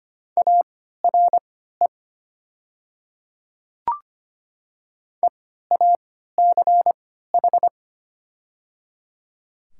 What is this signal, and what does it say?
Morse code, heard as a keyed tone, about 25 wpm, spaced as for 14 wpm, 700 Hz. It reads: ARE EACH
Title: 3x Character Spacing - Top 100 Words - 25wpm